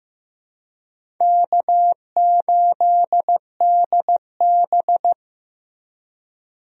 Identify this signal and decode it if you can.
Morse code, heard as a keyed tone, about 15 wpm, 700 Hz. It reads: K8DB
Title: USA Call Signs - 4 Characters - Repeated 2x - 15wpm